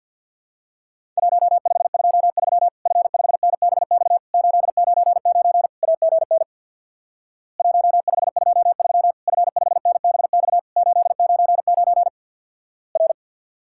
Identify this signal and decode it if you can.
Morse code, heard as a keyed tone, about 50 wpm, 700 Hz. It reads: 1523 F5NBX 899 AGN 1523 F5NBX 899 R
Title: Worked All Europe DX Contest - QTC Exchange with Repeats - 50wpm